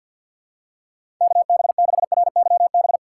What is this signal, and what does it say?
Morse code, heard as a keyed tone, about 51 wpm, 690 Hz. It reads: KB6RYB
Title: USA Call Signs - 2x Character Spacing - 50wpm